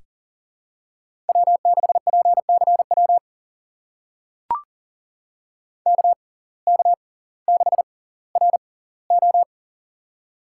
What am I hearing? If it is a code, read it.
Morse code, heard as a keyed tone, about 40 wpm, spaced as for 20 wpm, 700 Hz. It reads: W6PCW KK6RO
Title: USA Call Signs - 5 Characters - 6x Character Spacing - 40wpm